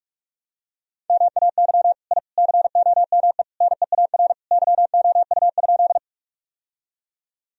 Morse code, heard as a keyed tone, about 46 wpm, 700 Hz. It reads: MAY I COME NEAR YOU?
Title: Sentences from Top 1000 Words - Review Part 1 - 45wpm